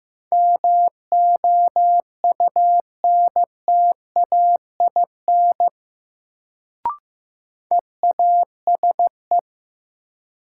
Morse code, heard as a keyed tone, about 16 wpm, 700 Hz. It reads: MOUNTAIN EASE
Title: Sentences from Top 1000 Words - Review Part 1 - New Word Drill - 15wpm